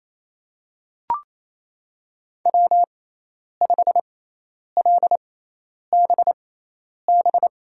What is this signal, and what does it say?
Morse code, heard as a keyed tone, about 28 wpm, spaced as for 10 wpm, 700 Hz. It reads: W5LBB
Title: USA Call Signs - 5 Characters - 6x Character Spacing - 28wpm